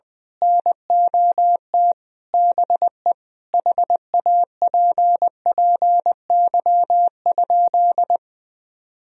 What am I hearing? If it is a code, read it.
Morse code, heard as a keyed tone, about 20 wpm, 700 Hz. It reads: NOT BE HAPPY?
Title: Sentences from Top 1000 Words - Review Part 2 - 20wpm